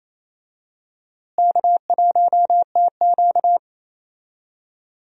Morse code, heard as a keyed tone, about 28 wpm, 700 Hz. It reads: K1TQ
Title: USA Call Signs - 4 Characters - Repeated 2x - 28wpm